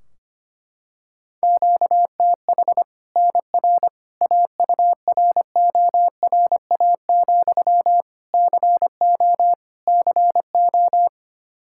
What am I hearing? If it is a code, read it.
Morse code, heard as a keyed tone, about 25 wpm, 700 Hz. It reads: QTH NR AURORA, CO CO
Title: Practice QSO - Exchange 1 - Beginning - 25wpm